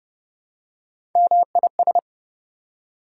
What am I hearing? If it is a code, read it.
Morse code, heard as a keyed tone, about 31 wpm, 700 Hz. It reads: MIS
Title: Prefixes and Suffixes 30wpm